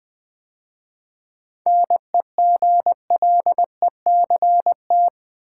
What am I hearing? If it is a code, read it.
Morse code, heard as a keyed tone, about 20 wpm, 700 Hz. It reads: NEGLECT
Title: 7 Letter Words - First 3 Letters Given - 20wpm